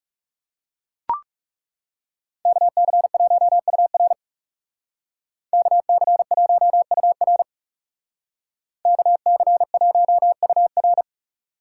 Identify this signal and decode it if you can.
Morse code, heard as a keyed tone, about 36 wpm, 700 Hz. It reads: KC1UR KC1UR KC1UR
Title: Speed Racing - USA Call Signs 30wpm